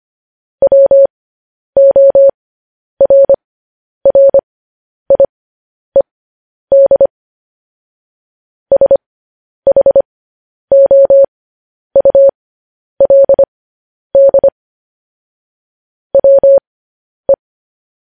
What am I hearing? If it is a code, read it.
Morse code, heard as a keyed tone, about 25 wpm, spaced as for 10 wpm, 550 Hz. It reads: WORRIED SHOULD WE
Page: news headlines in morse code at 10 wpm